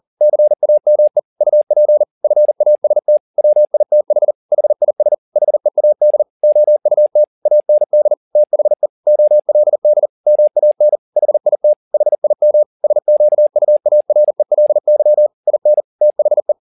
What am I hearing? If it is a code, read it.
Morse code, heard as a keyed tone, about 40 wpm, 600 Hz. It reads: CAME UP FAST WITH HIS HEAD OUT AND THE OLD MAN HIT HIM SQUARELY IN THE
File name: the old man and the sea 2 40wpm